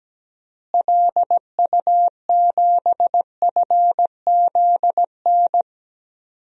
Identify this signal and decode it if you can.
Morse code, heard as a keyed tone, about 17 wpm, 700 Hz. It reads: LU7FZN
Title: DXCC-100 Call Signs - 6 Characters - Repeated 2x - 17wpm